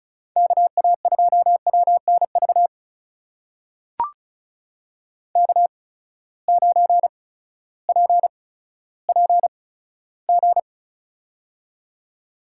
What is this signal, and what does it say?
Morse code, heard as a keyed tone, about 34 wpm, spaced as for 16 wpm, 700 Hz. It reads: KA2WNV K9PPG
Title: USA Call Signs - 8x Character Spacing - 35wpm